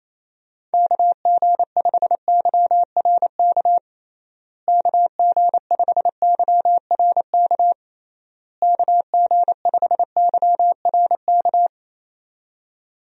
Morse code, heard as a keyed tone, about 28 wpm, 700 Hz. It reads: KG5YRK KG5YRK KG5YRK
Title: USA Call Signs - 6 Characters - Repeated 3x - 28wpm